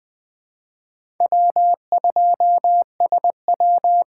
Morse code, heard as a keyed tone, about 20 wpm, 700 Hz. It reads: W2SW
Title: USA Call Signs - 4 Characters - 2x Character Spacing - 20wpm